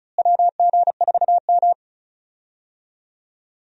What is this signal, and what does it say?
Morse code, heard as a keyed tone, about 34 wpm, 700 Hz. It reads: WG4M
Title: USA Call Signs - 4 Characters - Repeated 2x - 35wpm